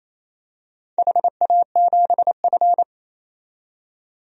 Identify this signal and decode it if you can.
Morse code, heard as a keyed tone, about 28 wpm, 690 Hz. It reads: HA7F
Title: DXCC-100 Call Signs - 4 Characters - 28wpm